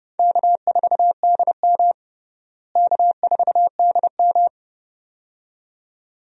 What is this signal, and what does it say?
Morse code, heard as a keyed tone, about 30 wpm, 700 Hz. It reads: K4DM K4DM
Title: USA Call Signs - 4 Characters - Repeated 2x - 30wpm